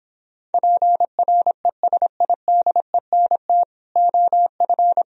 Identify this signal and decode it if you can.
Morse code, heard as a keyed tone, about 26 wpm, 700 Hz. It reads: PRESIDENT OF